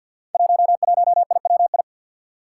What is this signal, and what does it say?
Morse code, heard as a keyed tone, about 46 wpm, 700 Hz. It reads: 11IWI
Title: Five Challenging Letter-Number Combinations 50wpm